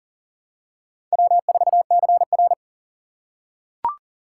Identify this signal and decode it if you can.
Morse code, heard as a keyed tone, about 40 wpm, 700 Hz. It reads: W4CR
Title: USA Call Signs - 4 Characters - 3x Character Spacing - 40wpm